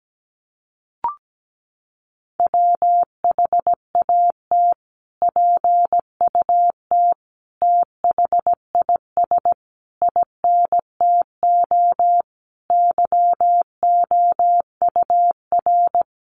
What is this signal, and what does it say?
Morse code, heard as a keyed tone, about 17 wpm, 700 Hz. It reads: WHAT PUT THIS INTO YOUR
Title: Sentences from Top 600 Words 17wpm